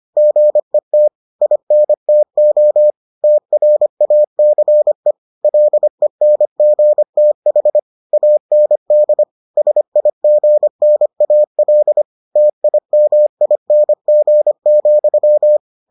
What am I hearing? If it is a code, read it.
Morse code, heard as a keyed tone, about 25 wpm, 600 Hz. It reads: GET INTO TRACE LENGTH AND SIGNAL TIMING,